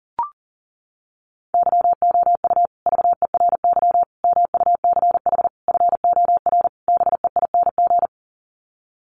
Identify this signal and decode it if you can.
Morse code, heard as a keyed tone, about 40 wpm, 700 Hz. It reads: YOU VERY MUCH FOR BEING
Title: Sets of 5 Words Encore 40wpm